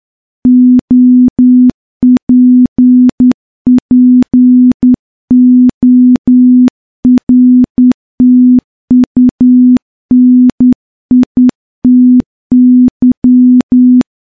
Morse code, heard as a keyed tone, about 10 wpm, 250 Hz. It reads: OPPORTUNITY
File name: morse-training-10WPM